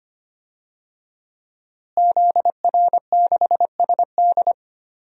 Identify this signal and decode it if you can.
Morse code, heard as a keyed tone, about 25 wpm, 700 Hz. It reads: ZR6SD
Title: DXCC-100 Call Signs - 5 Characters - Repeated 3x - 25wpm